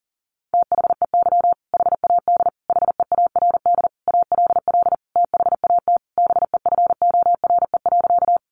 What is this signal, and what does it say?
Morse code, heard as a keyed tone, about 40 wpm, 700 Hz. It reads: THEY HAD HEARD ALL THAT BEFORE.
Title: Sentences from Top 400 Words 40wpm